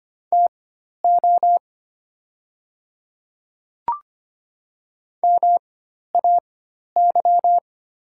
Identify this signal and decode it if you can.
Morse code, heard as a keyed tone, about 25 wpm, spaced as for 12 wpm, 700 Hz. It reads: TO MAY
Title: 4x Character Spacing - Top 100 Words - 25wpm